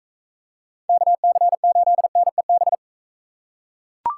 Morse code, heard as a keyed tone, about 41 wpm, 700 Hz. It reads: KC8NEB